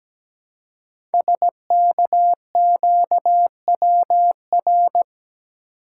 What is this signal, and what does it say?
Morse code, heard as a keyed tone, about 17 wpm, 700 Hz. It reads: SKQWR
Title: Code Groups - 5 Letter:Numbers - 17wpm